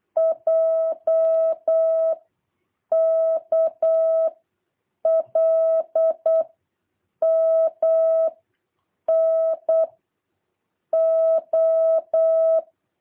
Morse code, heard as a keyed tone, about 8 wpm, 650 Hz. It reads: J K L M N O
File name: jh8cbh / 2023cw